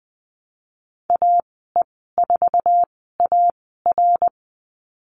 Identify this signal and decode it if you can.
Morse code, heard as a keyed tone, about 20 wpm, spaced as for 15 wpm, 700 Hz. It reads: AE4AR
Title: 076 Lesson 13g - Letter L - Call Signs 20wpm